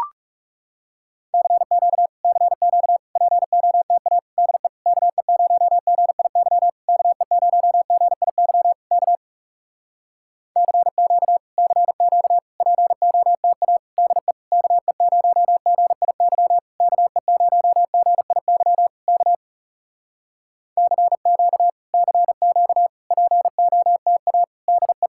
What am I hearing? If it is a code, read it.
Morse code, heard as a keyed tone, about 41 wpm, 700 Hz. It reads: CQ CQ POTA DE KE0GIY KE0GIY K CQ CQ POTA DE KE0GIY KE0GIY K CQ CQ POTA DE